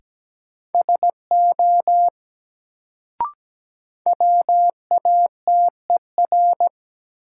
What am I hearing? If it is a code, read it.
Morse code, heard as a keyed tone, about 17 wpm, 700 Hz. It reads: SO WATER
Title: Sentences from Top 100 Words - New Word Drill - 17wpm